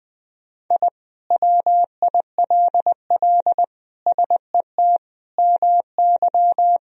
Morse code, heard as a keyed tone, about 20 wpm, 700 Hz. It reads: I WILL SET MY
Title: Sentences from Top 1000 Words - Review Part 2 - 20wpm